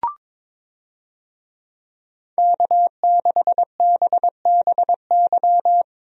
Morse code, heard as a keyed tone, about 22 wpm, 700 Hz. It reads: K6BBY
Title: USA Call Signs - 5 Characters - Repeated 2x - 22wpm